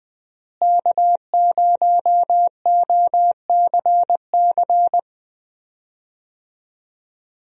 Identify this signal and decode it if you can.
Morse code, heard as a keyed tone, about 20 wpm, 700 Hz. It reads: K0OCC
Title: USA Call Signs - 5 Characters - Repeated 2x - 20wpm